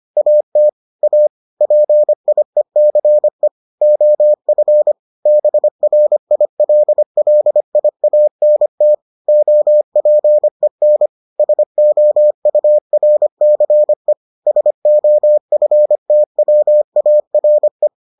Morse code, heard as a keyed tone, about 25 wpm, 600 Hz. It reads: AT A PIECE OF BRILLIANT OPEN SOURCE SOFTWARE